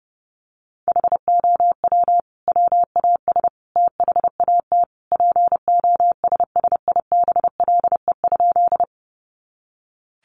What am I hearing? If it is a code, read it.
Morse code, heard as a keyed tone, about 30 wpm, 700 Hz. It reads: HOW WAS THAT POSSIBLE?
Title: Sentences from Top 500 Words 30wpm